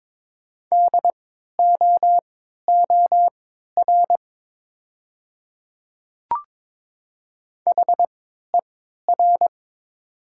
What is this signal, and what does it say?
Morse code, heard as a keyed tone, about 22 wpm, spaced as for 12 wpm, 700 Hz. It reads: DOOR HER
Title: 3x Character Spacing - Top 300 Words - 22wpm